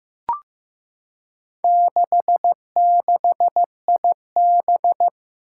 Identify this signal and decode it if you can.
Morse code, heard as a keyed tone, about 15 wpm, 700 Hz. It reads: 66IB
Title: Four Challenging Letter-Number Combinations 15wpm